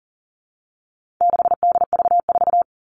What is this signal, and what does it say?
Morse code, heard as a keyed tone, about 40 wpm, 700 Hz. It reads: 6DV4
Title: All Cut Numbers - 4-Digit 40wpm